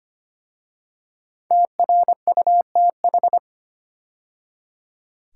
Morse code, heard as a keyed tone, about 25 wpm, 700 Hz. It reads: TRUTH
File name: 5 Letter Words - First 3 Letters Given - 25wpm